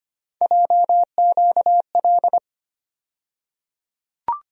Morse code, heard as a keyed tone, about 25 wpm, 700 Hz. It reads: JQL
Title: USA Call Sign Postfixes - 3 Letters 25wpm